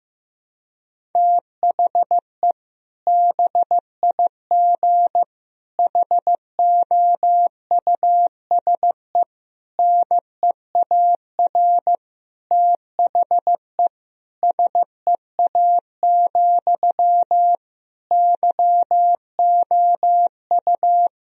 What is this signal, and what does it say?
Morse code, heard as a keyed tone, about 15 wpm, 700 Hz. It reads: THE BIG HOUSE NEAR THE SEA, YOU
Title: Sentences from Top 1000 Words - Review Part 1 - 15wpm